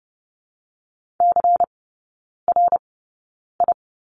Morse code, heard as a keyed tone, about 30 wpm, spaced as for 9 wpm, 700 Hz. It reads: CRI
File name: Lesson 18e - Letter m - Three Characters 30wpm